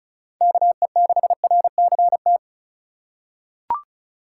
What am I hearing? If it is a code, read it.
Morse code, heard as a keyed tone, about 34 wpm, 700 Hz. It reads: KE6RCT
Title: USA Call Signs - 6 Characters - 2x Character Spacing - 35wpm